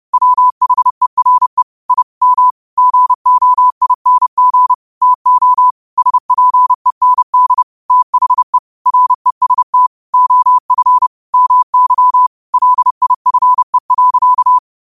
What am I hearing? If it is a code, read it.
Morse code, heard as a keyed tone, about 30 wpm, 1000 Hz. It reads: WHERE I M GOING TO SPEND THE REST OF MY LIFE.
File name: qotdfiles / 30wpm